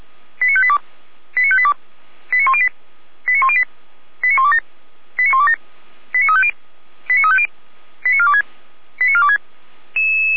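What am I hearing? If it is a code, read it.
Five-tone selective call (ZVEI), heard as quick runs of five tones, tones of about 70 ms. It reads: 89582 89582 89108 89108 89127 89127 89350 89350 89426 89426
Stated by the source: ZVEI five-tone selective call 89409 received on SDR.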